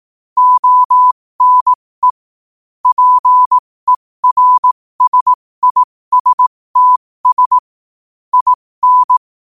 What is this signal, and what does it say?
Morse code, heard as a keyed tone, about 18 wpm, spaced as for 15 wpm, 1000 Hz. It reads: ONE PERSISTS IN
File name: qotdfiles / 15wpm